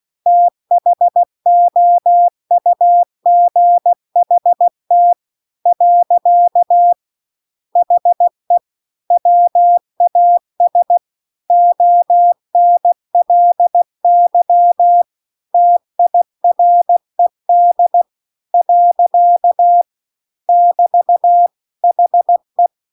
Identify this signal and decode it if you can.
Morse code, heard as a keyed tone, about 16 wpm, 700 Hz. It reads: THOUGHT . HE WAS ONLY TIRED . <BT> HE